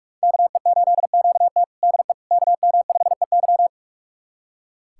Morse code, heard as a keyed tone, about 46 wpm, 700 Hz. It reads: KE8QT DE KM5EY
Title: USA Callsign DE Callsign - 5 Characters - 45wpm